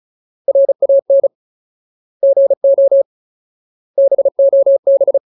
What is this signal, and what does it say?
Morse code, heard as a keyed tone, about 34 wpm, 550 Hz. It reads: RAN GO BOB
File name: Ultimate Word List / 35WPM Ultimate